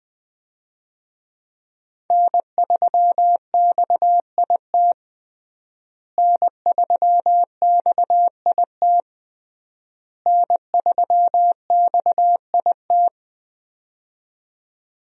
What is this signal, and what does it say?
Morse code, heard as a keyed tone, about 20 wpm, 700 Hz. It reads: N3XIT N3XIT N3XIT
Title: USA Call Signs - 5 Characters - Repeated 3x - 20wpm